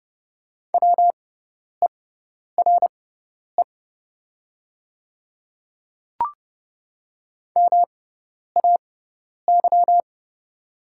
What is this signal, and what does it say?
Morse code, heard as a keyed tone, about 30 wpm, spaced as for 10 wpm, 700 Hz. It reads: WERE MAY
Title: 6x Character Spacing - Top 500 Words - 30wpm